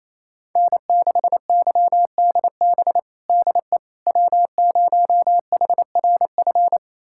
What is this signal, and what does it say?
Morse code, heard as a keyed tone, about 28 wpm, 700 Hz. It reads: N6YDB DE W0HRF
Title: USA Callsign DE Callsign - 5 Characters - 28wpm